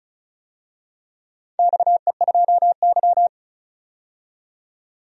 Morse code, heard as a keyed tone, about 36 wpm, 700 Hz. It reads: XE2Y
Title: DXCC-100 Call Signs - 4 Characters - 35wpm